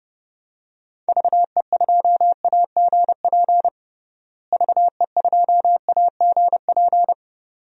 Morse code, heard as a keyed tone, about 30 wpm, 700 Hz. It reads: VE2AGP VE2AGP
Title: DXCC-100 Call Signs - 6 Characters - Repeated 2x - 30wpm